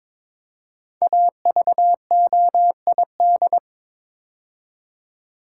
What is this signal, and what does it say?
Morse code, heard as a keyed tone, about 22 wpm, 700 Hz. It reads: AVOID
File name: 5 Letter Words - First 3 Letters Given - 22wpm